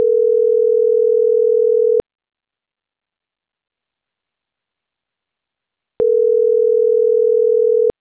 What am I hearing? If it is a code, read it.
Ringback tone (North America), 440 + 480 Hz.